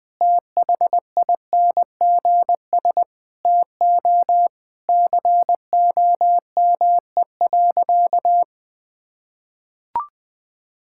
Morse code, heard as a keyed tone, about 21 wpm, 700 Hz. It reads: THINGS TO COME.
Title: Sentences with 6-Word Phrases - 20wpm